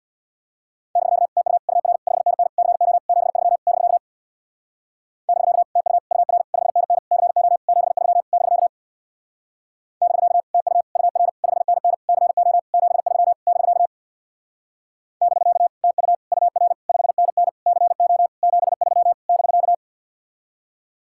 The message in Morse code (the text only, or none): <BK> TU UR 5NN CO 73 <BK> <BK> TU UR 5NN CO 73 <BK> <BK> TU UR 5NN CO 73 <BK> <BK> TU UR 5NN CO 73 <BK>